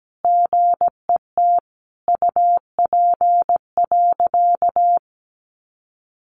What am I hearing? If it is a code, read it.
Morse code, heard as a keyed tone, about 17 wpm, 700 Hz. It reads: GET UP.
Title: Sentences from Top 500 Words 17wpm